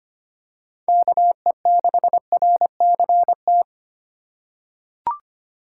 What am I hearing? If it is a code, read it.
Morse code, heard as a keyed tone, about 25 wpm, 700 Hz. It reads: KE6RCT
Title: USA Call Signs - 6 Characters - 2x Character Spacing - 25wpm